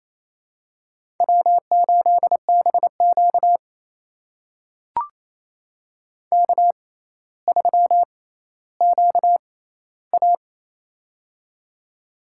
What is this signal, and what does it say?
Morse code, heard as a keyed tone, about 28 wpm, spaced as for 14 wpm, 700 Hz. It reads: W8BQ K3QA
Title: USA Call Signs - 4 Characters - 6x Character Spacing - 28wpm